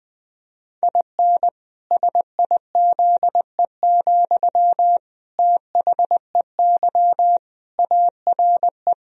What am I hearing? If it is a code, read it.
Morse code, heard as a keyed tone, about 20 wpm, 700 Hz. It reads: IN SIZE, THEY ARE
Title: Sentences from Top 1000 Words - Review Part 2 - 20wpm